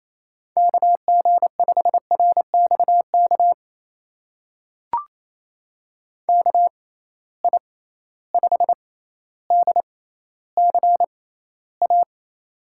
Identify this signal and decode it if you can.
Morse code, heard as a keyed tone, about 28 wpm, spaced as for 14 wpm, 700 Hz. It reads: KG5RXK KI5DCA
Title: USA Call Signs - 6 Characters - 6x Character Spacing - 28wpm